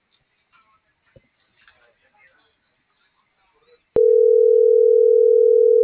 Ringback tone (North America), 440 + 480 Hz.